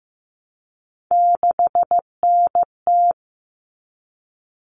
Morse code, heard as a keyed tone, about 15 wpm, 700 Hz. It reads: 6NT
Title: All Cut Numbers - 3-Digit 15wpm